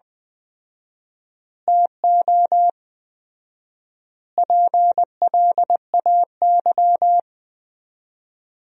Morse code, heard as a keyed tone, about 20 wpm, 700 Hz. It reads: TO PLAY
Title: 4x Word-Spacing - Sentences from Top 300 Words - 20wpm